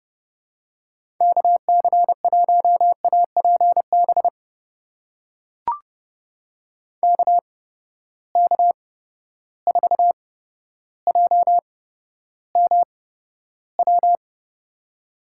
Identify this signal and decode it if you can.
Morse code, heard as a keyed tone, about 30 wpm, spaced as for 13 wpm, 700 Hz. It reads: KC1APB KK4JMW
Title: USA Call Signs - 6 Characters - 8x Character Spacing - 30wpm